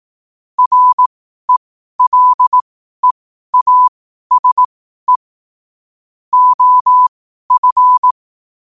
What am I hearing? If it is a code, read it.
Morse code, heard as a keyed tone, about 18 wpm, spaced as for 12 wpm, 1000 Hz. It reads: RELEASE OF
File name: qotdfiles / 12wpm